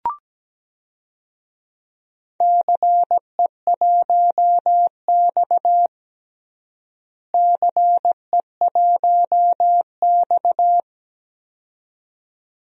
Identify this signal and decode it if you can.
Morse code, heard as a keyed tone, about 17 wpm, 700 Hz. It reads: CE1X CE1X
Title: DXCC-100 Call Signs - 4 Characters - Repeated 2x - 17wpm